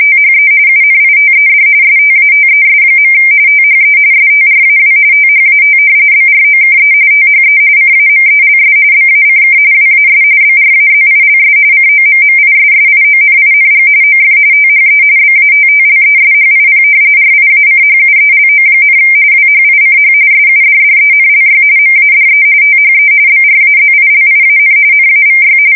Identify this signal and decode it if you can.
RTTY, about 45.45 baud. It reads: THE UNITED STATES. THE STRAW THAT BROKE THE CAMELS BACK WAS BRAZILS COMMISSIONING OF THE BATTLESHIP RIACHUELO, WHICH SUDDENLY MADE THE BRAZILIAN NAVY
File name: fsk170 / battleships